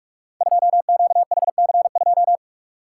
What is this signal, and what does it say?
Morse code, heard as a keyed tone, about 47 wpm, 700 Hz. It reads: 2QHC2
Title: Code Groups - 5 Letter:Numbers - 45wpm